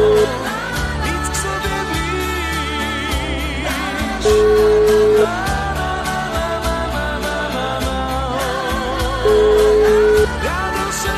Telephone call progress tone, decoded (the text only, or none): ringback tone